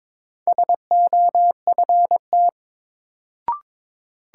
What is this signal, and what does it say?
Morse code, heard as a keyed tone, about 22 wpm, 700 Hz. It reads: SOFT
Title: Sentences from Top 1500 Words - New Word Drill Alt - 22wpm